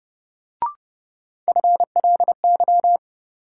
Morse code, heard as a keyed tone, about 30 wpm, 700 Hz. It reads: FLY
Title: Sentences from Top 500 Words - New Word Drill - 30wpm